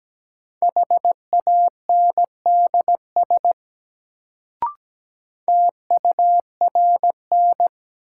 Morse code, heard as a keyed tone, about 17 wpm, 700 Hz. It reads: HANDS TURN
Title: Sets of 4 Words - New Word Drill - 17wpm